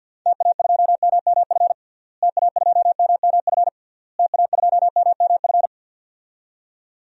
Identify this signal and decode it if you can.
Morse code, heard as a keyed tone, about 48 wpm, 700 Hz. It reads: TA2MMF TA2MMF TA2MMF
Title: DXCC-100 Call Signs - 6 Characters - Repeated 3x - 50wpm